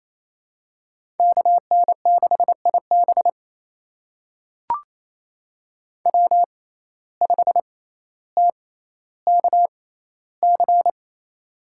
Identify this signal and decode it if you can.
Morse code, heard as a keyed tone, about 28 wpm, spaced as for 14 wpm, 700 Hz. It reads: KN6IB W5TKC
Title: USA Call Signs - 5 Characters - 6x Character Spacing - 28wpm